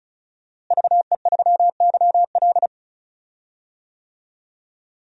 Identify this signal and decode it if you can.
Morse code, heard as a keyed tone, about 34 wpm, 700 Hz. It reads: VE3YL
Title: DXCC-100 Call Signs - 5 Characters - Repeated 3x - 35wpm